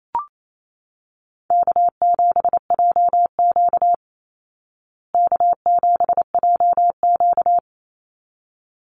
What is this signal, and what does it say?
Morse code, heard as a keyed tone, about 28 wpm, 700 Hz. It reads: K7JQ K7JQ
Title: Easy Call Signs Repeated 2x - 28wpm